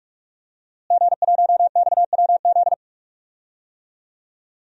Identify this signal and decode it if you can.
Morse code, heard as a keyed tone, about 42 wpm, 700 Hz. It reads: G1XWZ
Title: DXCC-100 Call Signs - 5 Characters - Repeated 3x - 45wpm